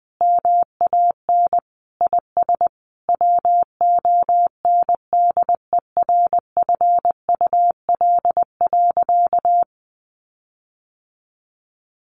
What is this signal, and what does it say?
Morse code, heard as a keyed tone, about 20 wpm, 700 Hz. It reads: MAN IS WONDERFUL.
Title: Sentences from Top 1400 Words 20wpm